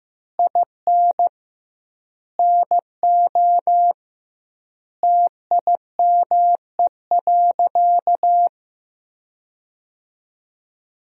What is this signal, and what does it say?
Morse code, heard as a keyed tone, about 15 wpm, 700 Hz. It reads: IN NO TIME.
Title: 2x Word-Spacing - Sentences from Top 400 Words - 15wpm